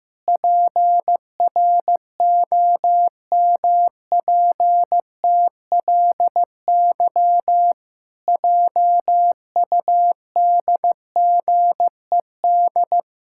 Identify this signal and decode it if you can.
Morse code, heard as a keyed tone, about 15 wpm, 700 Hz. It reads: PROMPTLY JUDGED
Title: Pangram Challenge v2 - 15wpm